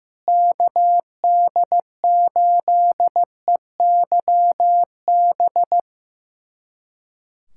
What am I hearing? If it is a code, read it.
Morse code, heard as a keyed tone, about 15 wpm, 700 Hz. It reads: KD8EYB
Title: USA Callsign DE Callsign - 6 Characters - 15wpm